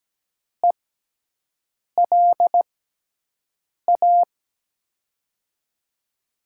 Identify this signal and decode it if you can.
Morse code, heard as a keyed tone, about 17 wpm, spaced as for 6 wpm, 700 Hz. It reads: ELA
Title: USA Call Signs - 6 Characters - 6x Character Spacing - 17wpm